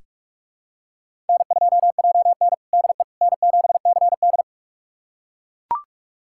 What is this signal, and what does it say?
Morse code, heard as a keyed tone, about 42 wpm, 700 Hz. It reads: N2JN DE N7CD
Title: USA Callsign DE Callsign - 4 Characters - 45wpm